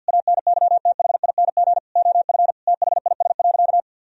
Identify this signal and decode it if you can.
Morse code, heard as a keyed tone, about 50 wpm, 700 Hz. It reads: ANYTHING OF THIS.
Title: Sentences from Top 1000 Words - Review Part 3 - 50wpm